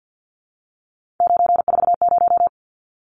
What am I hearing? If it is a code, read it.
Morse code, heard as a keyed tone, about 49 wpm, 700 Hz. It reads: 940
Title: Numbers - 3-Digit 50wpm